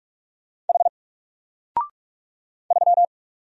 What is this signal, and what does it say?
Morse code, heard as a keyed tone, about 44 wpm, 700 Hz. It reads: H 3
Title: Single Letter-Number Alt 45wpm